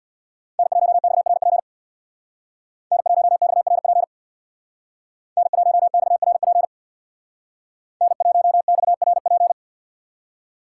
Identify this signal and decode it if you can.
Morse code, heard as a keyed tone, about 60 wpm, 700 Hz. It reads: N1XRP N1XRP N1XRP N1XRP
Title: Speed Racing - USA 1x3 Call Signs 50wpm